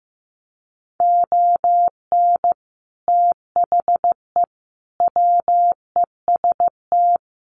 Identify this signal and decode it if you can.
Morse code, heard as a keyed tone, about 15 wpm, 700 Hz. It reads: ON THE WEST
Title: Sets of 5 Words Encore2 15wpm